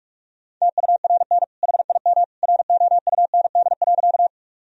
Morse code, heard as a keyed tone, about 45 wpm, 700 Hz. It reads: TURN HIM ROUND.
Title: Sentences from Top 1000 Words - Review Part 1 - 45wpm